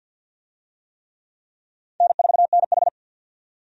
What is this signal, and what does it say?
Morse code, heard as a keyed tone, about 48 wpm, 700 Hz. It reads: N4NH
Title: USA Call Signs - 4 Characters - 50wpm